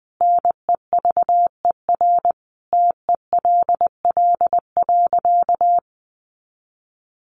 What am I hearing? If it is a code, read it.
Morse code, heard as a keyed tone, about 20 wpm, 700 Hz. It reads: NEVER TELL.
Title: Sentences from Top 300 Words 20wpm